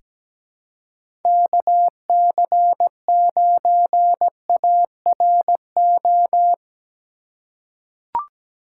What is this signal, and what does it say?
Morse code, heard as a keyed tone, about 17 wpm, 700 Hz. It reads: KC9ARO
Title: USA Call Signs - 6 Characters - 2x Character Spacing - 17wpm